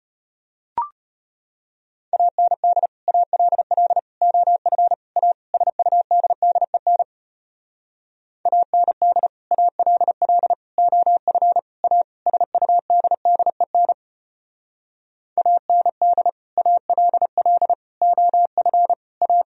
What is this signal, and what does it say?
Morse code, heard as a keyed tone, about 32 wpm, 700 Hz. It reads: AND ALL OF A SUDDEN AND ALL OF A SUDDEN AND ALL OF A